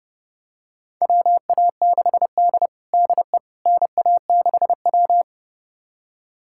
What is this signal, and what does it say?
Morse code, heard as a keyed tone, about 31 wpm, 700 Hz. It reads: WA6D DE NA6W
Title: USA Callsign DE Callsign - 4 Characters - 30wpm